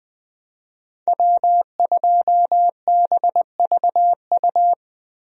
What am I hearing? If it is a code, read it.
Morse code, heard as a keyed tone, about 20 wpm, 700 Hz. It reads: W2BVU
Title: USA Call Signs - 5 Characters - Repeated 2x - 20wpm